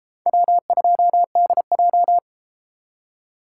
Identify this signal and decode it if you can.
Morse code, heard as a keyed tone, about 32 wpm, 700 Hz. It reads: W2DJ